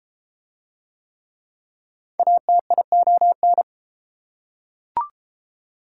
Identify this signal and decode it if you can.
Morse code, heard as a keyed tone, about 31 wpm, 700 Hz. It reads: ATION